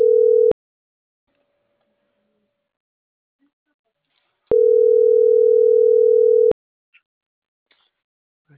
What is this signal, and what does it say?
Ringback tone (North America), 440 + 480 Hz.